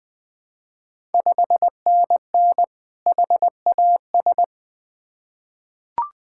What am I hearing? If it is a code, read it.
Morse code, heard as a keyed tone, about 20 wpm, 700 Hz. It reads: 5NN HAS
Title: Ontario QSO Party Exchange 20wpm